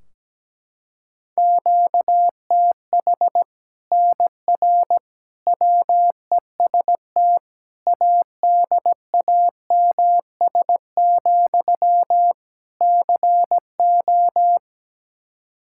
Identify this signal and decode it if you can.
Morse code, heard as a keyed tone, about 17 wpm, 700 Hz. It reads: QTH NR WEST ADAMS, CO
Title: Practice QSO - Exchange 1 - Beginning - 17wpm